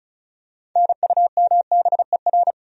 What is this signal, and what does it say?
Morse code, heard as a keyed tone, about 36 wpm, 700 Hz. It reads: NUMBER
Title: Top 100 Words 35wpm